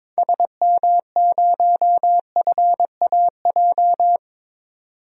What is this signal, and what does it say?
Morse code, heard as a keyed tone, about 22 wpm, 700 Hz. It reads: SM0FAJ
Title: DXCC-100 Call Signs - 6 Characters - 22wpm